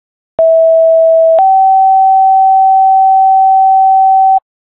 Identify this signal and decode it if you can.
Two-tone page, heard as a one-second tone, then a three-second tone, about 650 then 770 Hz. It is heard from 0.4 to 4.4 s.